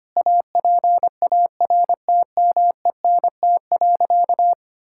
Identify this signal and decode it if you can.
Morse code, heard as a keyed tone, about 25 wpm, 700 Hz. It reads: APARTMENT.
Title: Sentences from Top 1000 Words - Review Part 6 - 25wpm